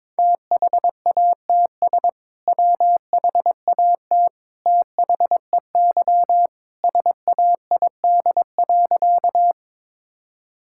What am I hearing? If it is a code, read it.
Morse code, heard as a keyed tone, about 22 wpm, 700 Hz. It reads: THATS WHAT THEY SAID.
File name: practice / 22wpm / Sentences with 2-Word Phrases - 22wpm